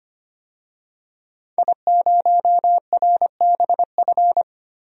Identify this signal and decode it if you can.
Morse code, heard as a keyed tone, about 25 wpm, 700 Hz. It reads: I0RBF
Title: DXCC-100 Call Signs - 5 Characters - Repeated 2x - 25wpm